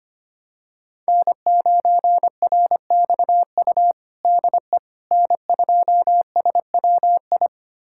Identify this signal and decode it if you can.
Morse code, heard as a keyed tone, about 25 wpm, 700 Hz. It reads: N9RXU DE N2SWI
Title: USA Callsign DE Callsign - 5 Characters - 25wpm